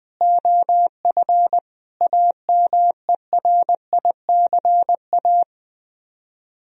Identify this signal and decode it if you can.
Morse code, heard as a keyed tone, about 21 wpm, 700 Hz. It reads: OF AMERICA